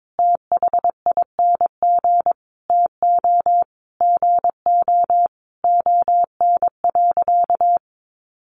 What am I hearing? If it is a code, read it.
Morse code, heard as a keyed tone, about 22 wpm, 700 Hz. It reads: THING TO GO ON.
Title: Sentences from Top 1300 Words 22wpm